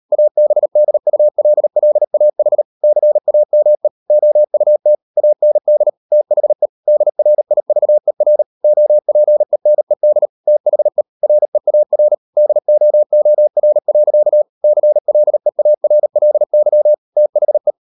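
Morse code, heard as a keyed tone, about 38 wpm, 600 Hz. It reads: ABDULLAH CAME OUT AND THE DRIVER OPENED THE REAR DOOR. CLEARLY THE
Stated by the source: Morse code practice at 38 WPM, sending Ken Follett's novel Never in English, from CW med Gnister.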